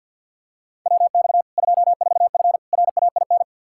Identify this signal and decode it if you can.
Morse code, heard as a keyed tone, about 52 wpm, 700 Hz. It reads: WX 24F RAIN
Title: Practice QSO - Exchange 2 - Info - 50wpm